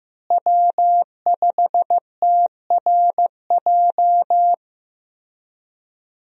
Morse code, heard as a keyed tone, about 15 wpm, 700 Hz. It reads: W5TRJ
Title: USA Call Signs - 5 Characters - Repeated 3x - 15wpm